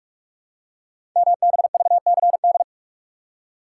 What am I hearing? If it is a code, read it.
Morse code, heard as a keyed tone, about 44 wpm, 700 Hz. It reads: MBVCD
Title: Code Groups - 5 Letters - 45wpm